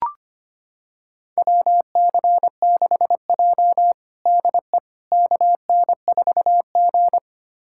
Morse code, heard as a keyed tone, about 25 wpm, 700 Hz. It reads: WC6J DE KN4G
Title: USA Callsign DE Callsign - 4 Characters - 25wpm